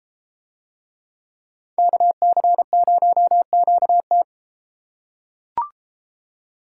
Morse code, heard as a keyed tone, about 33 wpm, 700 Hz. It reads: KC0QT